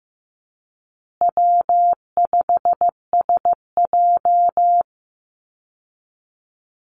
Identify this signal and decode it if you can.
Morse code, heard as a keyed tone, about 15 wpm, 700 Hz. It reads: W5SJ
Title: Easy Call Signs Repeated 3x - 15wpm